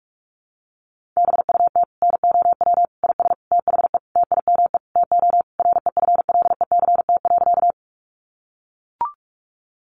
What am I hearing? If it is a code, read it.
Morse code, heard as a keyed tone, about 44 wpm, 700 Hz. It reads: BUT NOW IS THE TIME TO REFLECT.
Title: Sentences from Top 1500 Words 45wpm